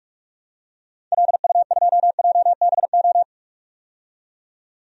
Morse code, heard as a keyed tone, about 41 wpm, 700 Hz. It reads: LU2JBO
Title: DXCC-100 Call Signs - 6 Characters - 45wpm